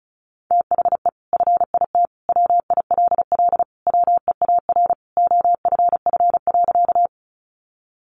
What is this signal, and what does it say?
Morse code, heard as a keyed tone, about 34 wpm, 700 Hz. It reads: THE FIT WILL WEAR OFF.
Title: Sentences from Top 1500 Words 35wpm